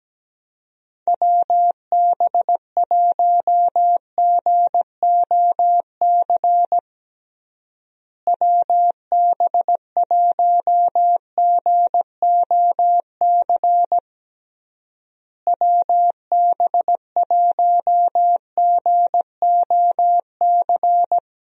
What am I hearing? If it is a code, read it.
Morse code, heard as a keyed tone, about 17 wpm, 700 Hz. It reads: WB1GOC WB1GOC WB1GOC
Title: USA Call Signs - 6 Characters - Repeated 3x - 17wpm